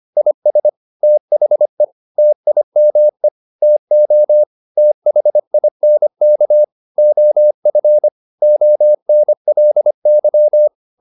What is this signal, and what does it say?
Morse code, heard as a keyed tone, about 25 wpm, 600 Hz. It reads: IS THE TIME TO THINK OF ONLY
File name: the old man and the sea part 1 25wpm